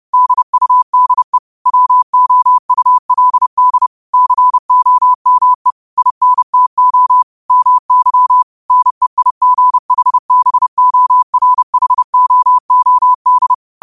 Morse code, heard as a keyed tone, about 30 wpm, 1000 Hz. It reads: DUDE WOULD COME INTO MY NEIGHBORHOOD
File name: qotdfiles / 30wpm